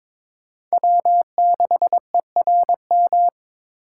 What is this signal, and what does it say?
Morse code, heard as a keyed tone, about 21 wpm, 700 Hz. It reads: W6ERM
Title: USA Call Signs - 5 Characters - 3x Character Spacing - 22wpm